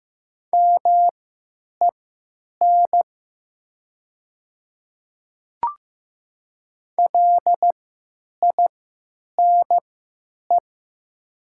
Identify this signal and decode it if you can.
Morse code, heard as a keyed tone, about 15 wpm, spaced as for 8 wpm, 700 Hz. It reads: MEN LINE
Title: 3x Character Spacing - Top 200 Words - 15wpm